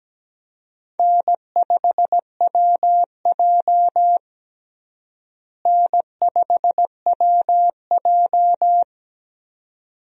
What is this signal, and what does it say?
Morse code, heard as a keyed tone, about 17 wpm, 700 Hz. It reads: N5WJ N5WJ
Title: USA Call Signs - 4 Characters - Repeated 2x - 17wpm